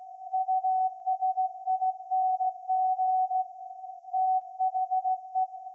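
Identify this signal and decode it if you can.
Morse code, heard as a keyed tone, about 16 wpm, 750 Hz. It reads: USING THE
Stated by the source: Morse code from SAQ, received at a station in Putignano, Apulia, Italy, on VLF.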